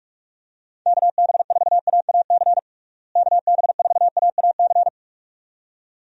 Morse code, heard as a keyed tone, about 42 wpm, 700 Hz. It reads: KB4AAC KB4AAC
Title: USA Call Signs - 6 Characters - Repeated 2x - 45wpm